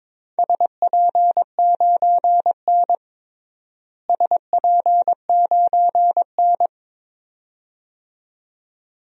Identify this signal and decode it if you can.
Morse code, heard as a keyed tone, about 22 wpm, 700 Hz. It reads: SP9N SP9N
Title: DXCC-100 Call Signs - 4 Characters - Repeated 2x - 22wpm